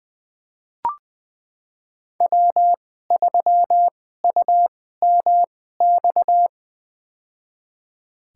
Morse code, heard as a keyed tone, about 20 wpm, spaced as for 15 wpm, 700 Hz. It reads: W3UMX
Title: USA Call Signs - 5 Characters - 2x Character Spacing - 20wpm